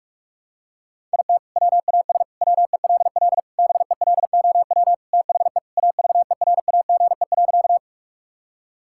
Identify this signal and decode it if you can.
Morse code, heard as a keyed tone, about 45 wpm, 700 Hz. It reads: IT WAS WELL BELOW THE AVERAGE.
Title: Sentences from Top 1000 Words - Review Part 6 - 45wpm